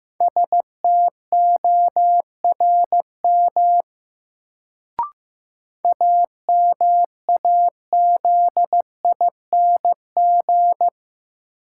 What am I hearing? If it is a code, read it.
Morse code, heard as a keyed tone, about 15 wpm, 700 Hz. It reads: STORM AMAZING
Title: Sentences from Top 1500 Words - New Word Drill - 15wpm